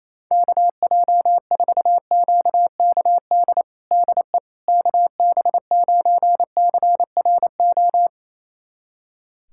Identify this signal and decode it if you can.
Morse code, heard as a keyed tone, about 28 wpm, 700 Hz. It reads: KJ4QKD DE KB9CRO
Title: USA Callsign DE Callsign - 6 Characters - 28wpm